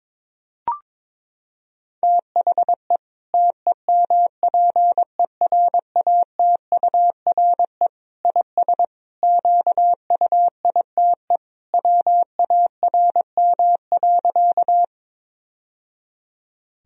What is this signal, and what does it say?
Morse code, heard as a keyed tone, about 22 wpm, 700 Hz. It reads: THE TEMPERATURE IS QUITE WARM.
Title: Sentences from Top 1000 Words - Review Part 4 - 22wpm